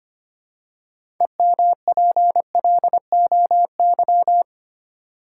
Morse code, heard as a keyed tone, about 25 wpm, 700 Hz. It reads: EMPLOY
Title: Six Letter Words - 25wpm